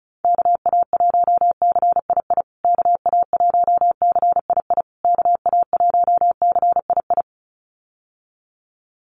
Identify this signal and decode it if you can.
Morse code, heard as a keyed tone, about 34 wpm, 700 Hz. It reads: KA1CII KA1CII KA1CII
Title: USA Call Signs Repeated 3x 35wpm